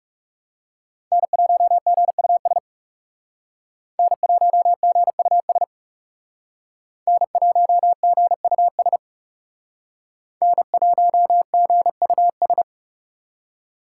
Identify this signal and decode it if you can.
Morse code, heard as a keyed tone, about 38 wpm, 700 Hz. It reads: N1GUS N1GUS N1GUS N1GUS